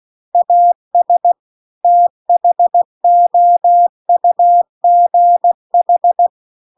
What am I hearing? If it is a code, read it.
Morse code, heard as a keyed tone, about 16 wpm, 700 Hz. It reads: AS THOUGH
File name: CW-The-Big-Twoheaded-River-2-16WPM